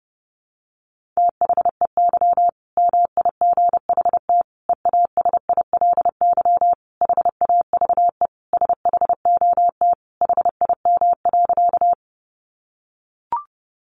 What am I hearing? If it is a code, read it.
Morse code, heard as a keyed tone, about 30 wpm, 700 Hz. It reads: THEY MIGHT EASILY HAVE SHOT HIM.
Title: Sentences from Top 1100 Words 30wpm